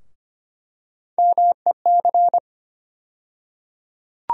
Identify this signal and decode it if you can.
Morse code, heard as a keyed tone, about 25 wpm, 700 Hz. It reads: MEC
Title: USA Call Sign Postfixes - 3 Letters 25wpm